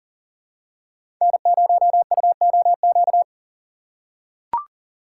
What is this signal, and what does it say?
Morse code, heard as a keyed tone, about 40 wpm, 700 Hz. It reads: N0UOQ
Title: USA Call Signs - 5 Characters - 6x Character Spacing - 40wpm